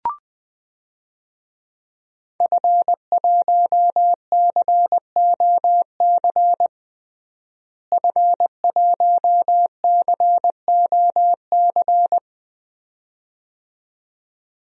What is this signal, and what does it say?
Morse code, heard as a keyed tone, about 20 wpm, 700 Hz. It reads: F1COC F1COC
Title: DXCC-100 Call Signs - 5 Characters - Repeated 2x - 20wpm